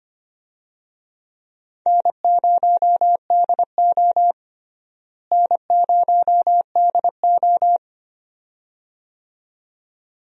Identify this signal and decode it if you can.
Morse code, heard as a keyed tone, about 25 wpm, 700 Hz. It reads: N0DO N0DO
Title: USA Call Signs - 4 Characters - Repeated 2x - 25wpm